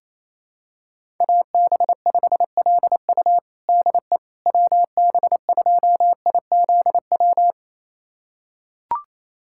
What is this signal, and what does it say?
Morse code, heard as a keyed tone, about 28 wpm, 700 Hz. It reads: AB5LU DE WB2IZW
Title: USA Callsign DE Callsign 28wpm